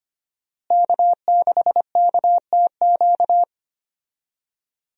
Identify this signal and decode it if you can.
Morse code, heard as a keyed tone, about 25 wpm, 700 Hz. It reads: K6KTQ
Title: USA Call Signs - 5 Characters - Repeated 3x - 25wpm